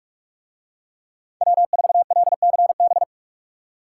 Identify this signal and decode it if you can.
Morse code, heard as a keyed tone, about 43 wpm, 700 Hz. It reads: W4LCB